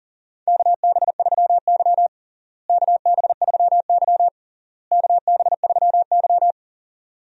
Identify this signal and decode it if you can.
Morse code, heard as a keyed tone, about 40 wpm, 700 Hz. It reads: KB3Y KB3Y KB3Y
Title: USA Call Signs - 4 Characters - Repeated 3x - 40wpm